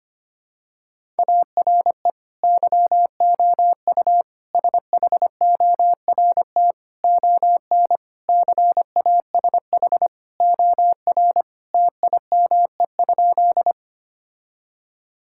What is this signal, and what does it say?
Morse code, heard as a keyed tone, about 25 wpm, 700 Hz. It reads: ARE YOU SHORT ON CASH OR TIME?
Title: Sentences from Top 1000 Words - Review Part 5 - 25wpm